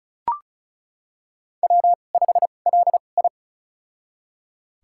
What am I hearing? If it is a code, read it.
Morse code, heard as a keyed tone, about 36 wpm, spaced as for 25 wpm, 700 Hz. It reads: W5LI
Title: USA Call Signs - 4 Characters - 2x Character Spacing - 35wpm